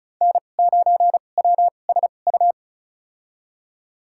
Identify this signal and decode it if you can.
Morse code, heard as a keyed tone, about 36 wpm, spaced as for 25 wpm, 700 Hz. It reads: N9WSU
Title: USA Call Signs - 5 Characters - 2x Character Spacing - 35wpm